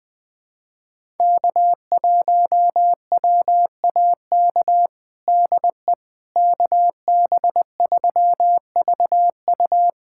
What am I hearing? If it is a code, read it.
Morse code, heard as a keyed tone, about 20 wpm, 700 Hz. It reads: K1WAK DE KB3VU
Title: USA Callsign DE Callsign - 5 Characters - 20wpm